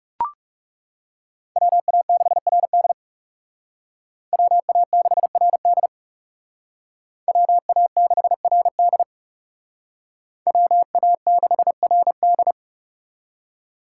Morse code, heard as a keyed tone, about 36 wpm, 700 Hz. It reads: WA6RD WA6RD WA6RD WA6RD
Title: Speed Racing - USA 2x2 Call Signs 30wpm